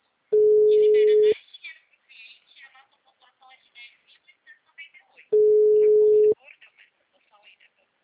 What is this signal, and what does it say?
Ringback tone (Europe, 425 Hz), with speech.